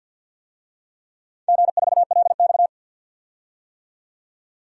Morse code, heard as a keyed tone, about 45 wpm, 700 Hz. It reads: G4LX
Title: DXCC-100 Call Signs - 4 Characters - 50wpm